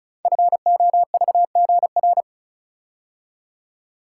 Morse code, heard as a keyed tone, about 36 wpm, 700 Hz. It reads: FOVGR
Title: Code Groups - 5 Letters - 35wpm